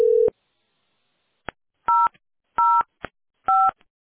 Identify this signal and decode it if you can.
DTMF, heard as telephone touch tones, tones of about 230 ms. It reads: 005